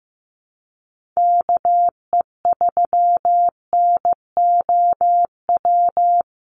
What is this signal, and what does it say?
Morse code, heard as a keyed tone, about 15 wpm, 700 Hz. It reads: KE3NOW
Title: USA Call Signs 15wpm